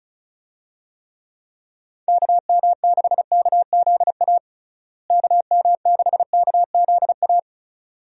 Morse code, heard as a keyed tone, about 36 wpm, 700 Hz. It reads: KM6KZA KM6KZA
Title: USA Call Signs - 6 Characters - Repeated 2x - 35wpm